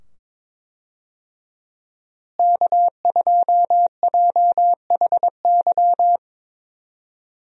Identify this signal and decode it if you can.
Morse code, heard as a keyed tone, about 22 wpm, 700 Hz. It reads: K2JHY
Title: USA Call Signs - 5 Characters - Repeated 2x - 22wpm